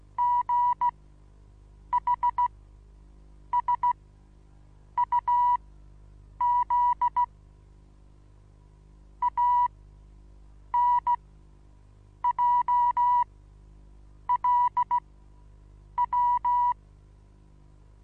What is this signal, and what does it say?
Morse code, heard as a keyed tone, about 16 wpm, spaced as for 7 wpm, 970 Hz. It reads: GHSUZ ANJLW
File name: Onlinelehrgang / morsen / DJ4UF Morsen25